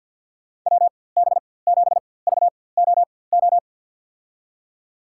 Morse code, heard as a keyed tone, about 50 wpm, spaced as for 23 wpm, 700 Hz. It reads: WB7VOO
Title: USA Call Signs - 6 Characters - 4x Character Spacing - 50wpm